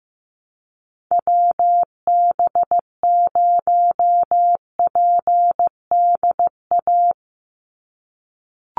Morse code, heard as a keyed tone, about 15 wpm, 700 Hz. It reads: WB0PDA
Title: USA Call Signs 15wpm